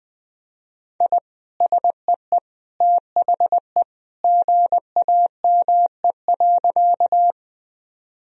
Morse code, heard as a keyed tone, about 21 wpm, 700 Hz. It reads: I SEE THE GAME.
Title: Sentences from Top 1000 Words - Review Part 3 - 20wpm